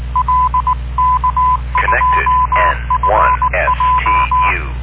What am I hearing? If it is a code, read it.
Morse code, heard as a keyed tone, about 19 wpm, 1010 Hz. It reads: L K2FJ